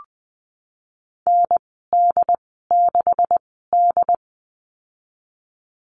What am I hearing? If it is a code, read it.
Morse code, heard as a keyed tone, about 20 wpm, spaced as for 15 wpm, 700 Hz. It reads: ND6D
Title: 161 Lesson 26g - Letter g - Call Signs 20wpm